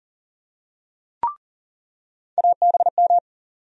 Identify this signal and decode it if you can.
Morse code, heard as a keyed tone, about 40 wpm, 700 Hz. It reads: ABM
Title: USA Call Sign Postfixes - 3 Letters 40wpm